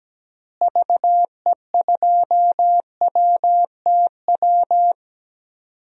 Morse code, heard as a keyed tone, about 17 wpm, 700 Hz. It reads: VE2WTW
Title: DXCC-100 Call Signs - 6 Characters - Repeated 2x - 17wpm